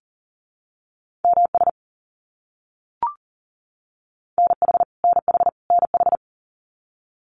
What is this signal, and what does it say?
Morse code, heard as a keyed tone, about 40 wpm, 700 Hz. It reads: MS NH NH NH
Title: US States Repeated 3x 40wpm